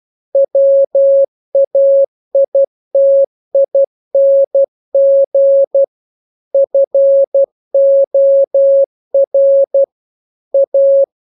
Morse code, heard as a keyed tone, about 12 wpm, 550 Hz. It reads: WAITING FOR A